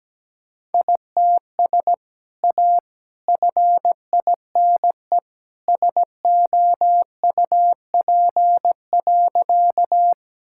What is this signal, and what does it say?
Morse code, heard as a keyed tone, about 17 wpm, 700 Hz. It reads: ITS A FINE SOUP.
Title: Sentences from Top 1000 Words - Review Part 6 - 17wpm